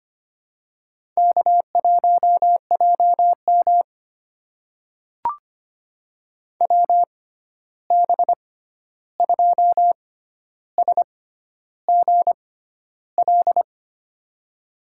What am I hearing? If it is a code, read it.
Morse code, heard as a keyed tone, about 25 wpm, spaced as for 11 wpm, 700 Hz. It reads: K1JM WB2SGL
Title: USA Call Signs - 6x Character Spacing - 25wpm